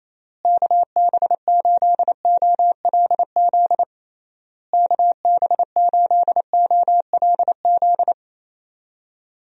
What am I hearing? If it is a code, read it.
Morse code, heard as a keyed tone, about 28 wpm, 700 Hz. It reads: KB8OLZ KB8OLZ
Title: USA Call Signs - 6 Characters - Repeated 3x - 28wpm